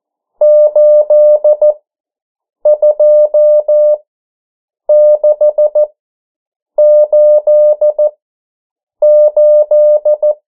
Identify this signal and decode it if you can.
Morse code, heard as a keyed tone, about 14 wpm, spaced as for 7 wpm, 600 Hz. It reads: 82688
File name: morse mp3 / 09 urok 8